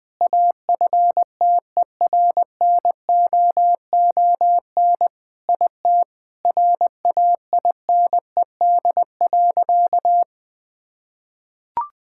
Morse code, heard as a keyed tone, about 20 wpm, 700 Hz. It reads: AFTERNOON IT RAINED.
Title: Sentences with 3-Word Phrases - 20wpm